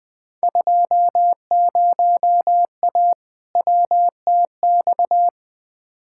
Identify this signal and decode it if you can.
Morse code, heard as a keyed tone, about 20 wpm, 700 Hz. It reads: 20A WTX
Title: Field Day Exchanges Encore - 20wpm